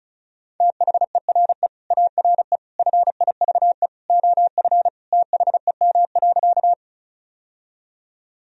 Morse code, heard as a keyed tone, about 36 wpm, 700 Hz. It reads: THERE ARE FIVE OF THEM.
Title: Sentences from Top 1000 Words - Review Part 2 - 35wpm